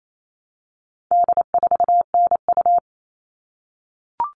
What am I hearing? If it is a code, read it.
Morse code, heard as a keyed tone, about 28 wpm, 700 Hz. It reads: D4NU
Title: All Cut Numbers - 4-Digit 28wpm